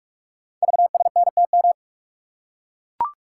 Morse code, heard as a keyed tone, about 44 wpm, 700 Hz. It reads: VSNTM
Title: Code Groups - 5 Letter:Numbers - 45wpm